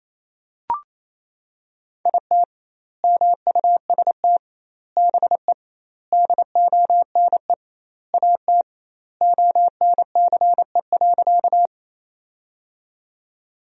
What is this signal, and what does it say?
Morse code, heard as a keyed tone, about 28 wpm, 700 Hz. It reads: IT MUST BE DONE AT ONCE.
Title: 2x Word-Spacing - Sentences from Top 500 Words - 28wpm